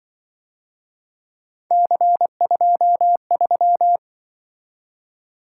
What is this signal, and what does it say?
Morse code, heard as a keyed tone, about 24 wpm, 700 Hz. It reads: C23
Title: Speed-Racing - Three Letter-Number Combinations - 20wpm